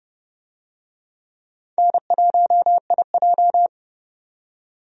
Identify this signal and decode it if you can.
Morse code, heard as a keyed tone, about 30 wpm, 700 Hz. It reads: N1IJ
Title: USA Call Signs - 4 Characters - 30wpm